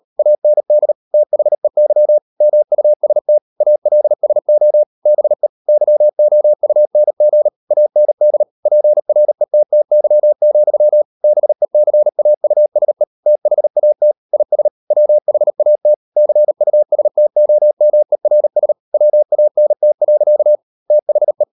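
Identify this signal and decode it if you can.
Morse code, heard as a keyed tone, about 38 wpm, 600 Hz. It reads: AND THEY MUST ALSO BE YOUNG AND PRETTY, BECAUSE THAT IS WHAT CUSTOMERS WANT. THE
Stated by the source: Morse code practice at 38 WPM, sending Ken Follett's novel Never in English, from CW med Gnister.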